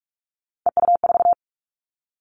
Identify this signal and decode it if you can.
Morse code, heard as a keyed tone, about 44 wpm, 700 Hz. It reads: EU4
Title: All Cut Numbers - 3-Digit 45wpm